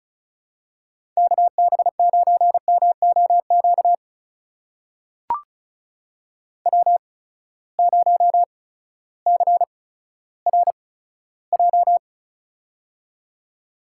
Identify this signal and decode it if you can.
Morse code, heard as a keyed tone, about 36 wpm, spaced as for 16 wpm, 700 Hz. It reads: KB9MOQ W0CRJ
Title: USA Call Signs - 8x Character Spacing - 35wpm